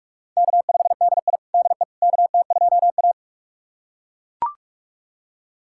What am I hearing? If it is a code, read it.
Morse code, heard as a keyed tone, about 42 wpm, 700 Hz. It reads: K5DI DE KT2A
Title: USA Callsign DE Callsign - 4 Characters - 45wpm